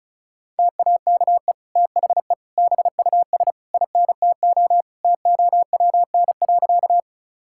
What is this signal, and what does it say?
Morse code, heard as a keyed tone, about 36 wpm, 700 Hz. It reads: TAKE THE BUS INTO TOWN.
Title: Sentences from Top 1000 Words - Review Part 6 - 35wpm